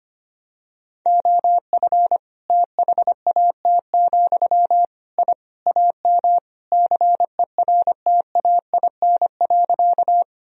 Morse code, heard as a keyed tone, about 25 wpm, 700 Hz. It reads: OF THAT, I AM CERTAIN.
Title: Sentences from Top 1000 Words - Review Part 2 - 25wpm